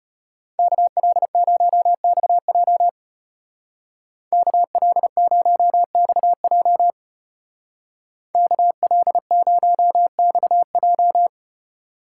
Morse code, heard as a keyed tone, about 32 wpm, 700 Hz. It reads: KL0XJ KL0XJ KL0XJ